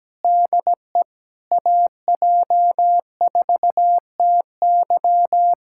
Morse code, heard as a keyed tone, about 17 wpm, 700 Hz. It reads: DE AJ4TY
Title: Speed-Racing Common QSO Phrases - 17wpm